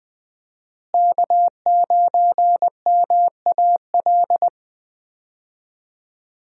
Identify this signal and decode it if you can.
Morse code, heard as a keyed tone, about 20 wpm, 700 Hz. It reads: K9MAL
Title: USA Call Signs - 5 Characters - Repeated 3x - 20wpm